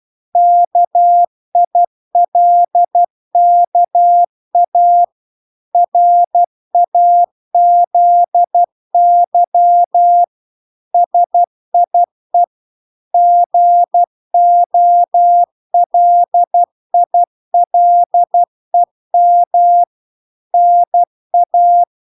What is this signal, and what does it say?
Morse code, heard as a keyed tone, about 12 wpm, 700 Hz. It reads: KILKA RAZY SIE GOLILEM NA